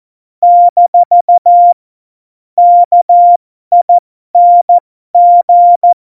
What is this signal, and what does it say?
Morse code, heard as a keyed tone, about 14 wpm, spaced as for 12 wpm, 700 Hz. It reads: - KING
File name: Morsekurs / Witze